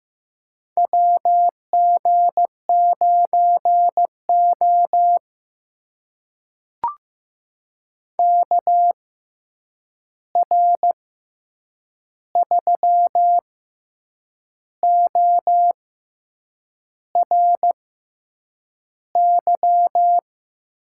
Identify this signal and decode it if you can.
Morse code, heard as a keyed tone, about 15 wpm, spaced as for 7 wpm, 700 Hz. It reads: WG9O KR3ORY
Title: USA Call Signs - 6x Character Spacing - 15wpm